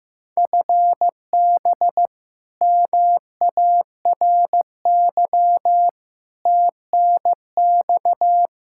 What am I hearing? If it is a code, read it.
Morse code, heard as a keyed tone, about 15 wpm, 700 Hz. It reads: FB MARY TNX
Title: Practice QSO - Exchange 3 - More Info - 15wpm